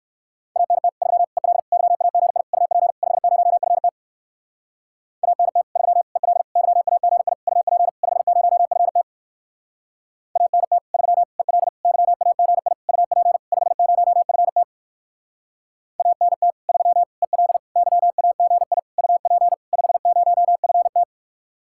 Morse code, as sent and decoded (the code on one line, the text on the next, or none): .- -. - / ...-- / . .-.. / -.-- .- --. .. / ..- .--. / ..... ----- ..-. - / .- -. - / ...-- / . .-.. / -.-- .- --. .. / ..- .--. / ..... ----- ..-. - / .- -. - / ...-- / . .-.. / -.-- .- --. .. / ..- .--. / ..... ----- ..-. - / .- -. - / ...-- / . .-.. / -.-- .- --. .. / ..- .--. / ..... ----- ..-. -
ANT 3 EL YAGI UP 50FT ANT 3 EL YAGI UP 50FT ANT 3 EL YAGI UP 50FT ANT 3 EL YAGI UP 50FT